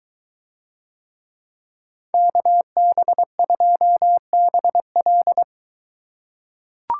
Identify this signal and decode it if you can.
Morse code, heard as a keyed tone, about 21 wpm, 700 Hz. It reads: KB2BL